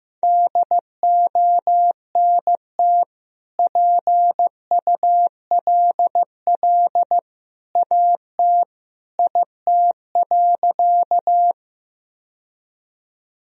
Morse code, heard as a keyed tone, about 15 wpm, 700 Hz. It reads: DONT PULL AT IT.
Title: Sentences from Top 1000 Words - Review Part 2 - 15wpm